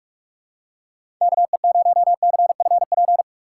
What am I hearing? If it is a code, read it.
Morse code, heard as a keyed tone, about 46 wpm, 700 Hz. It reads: KE0CFP
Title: USA Call Signs - 6 Characters - 4x Character Spacing - 45wpm